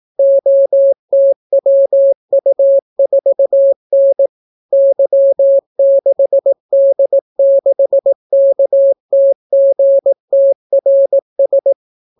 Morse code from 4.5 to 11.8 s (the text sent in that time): Y6D6KTGTRS